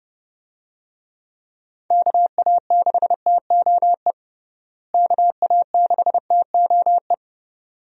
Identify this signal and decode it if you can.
Morse code, heard as a keyed tone, about 32 wpm, 700 Hz. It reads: KA6TOE KA6TOE
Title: USA Call Signs - 6 Characters - Repeated 2x - 30wpm